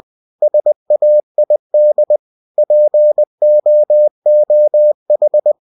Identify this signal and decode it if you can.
Morse code, heard as a keyed tone, about 20 wpm, 600 Hz. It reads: SAID POOH
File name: cw-books / winnie-the-pooh / winnie-20wpm-page-0065